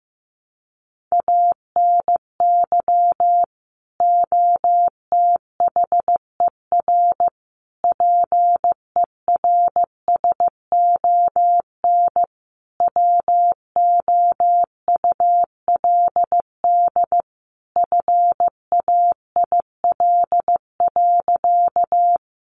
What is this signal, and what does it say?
Morse code, heard as a keyed tone, about 15 wpm, 700 Hz. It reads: ANY OTHER PERSON WOULD FAIL.
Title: Sentences from Top 1100 Words 15wpm